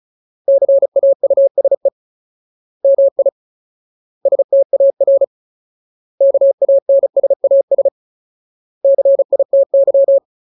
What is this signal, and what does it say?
Morse code, heard as a keyed tone, about 34 wpm, 550 Hz. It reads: CAUSE MI STAR KANSAS CITY
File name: Ultimate Word List / 35WPM Ultimate